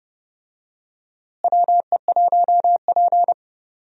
Morse code, heard as a keyed tone, about 30 wpm, 700 Hz. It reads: WE1P
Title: USA Call Signs - 4 Characters - Repeated 3x - 30wpm